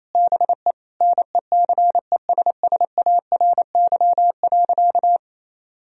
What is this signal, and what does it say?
Morse code, heard as a keyed tone, about 28 wpm, 700 Hz. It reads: BE NECESSARY.
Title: Sentences from Top 1000 Words - Review Part 3 - 28wpm